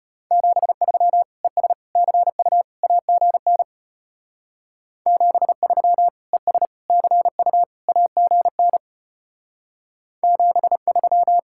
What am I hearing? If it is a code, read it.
Morse code, heard as a keyed tone, about 33 wpm, 700 Hz. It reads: 73 ES CU AGN 73 ES CU AGN 73